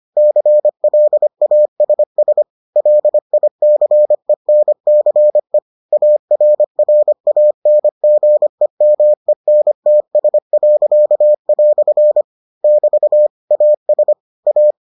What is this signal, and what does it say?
Morse code, heard as a keyed tone, about 25 wpm, 600 Hz. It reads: CLASS LICENCE ARRANGEMENTS." <BT> AS A